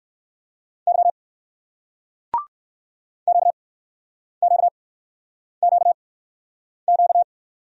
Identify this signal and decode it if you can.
Morse code, heard as a keyed tone, about 60 wpm, spaced as for 9 wpm, 700 Hz. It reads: Y QQQQ